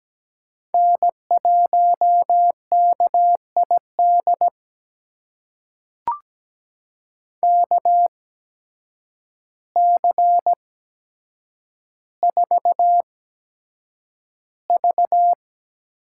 Morse code, heard as a keyed tone, about 17 wpm, spaced as for 8 wpm, 700 Hz. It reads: N1KID KC4V
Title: USA Call Signs - 8x Character Spacing - 17wpm